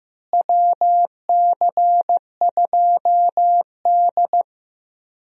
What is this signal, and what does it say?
Morse code, heard as a keyed tone, about 15 wpm, 700 Hz. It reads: WC2D
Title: USA Call Signs - 4 Characters - Repeated 3x - 15wpm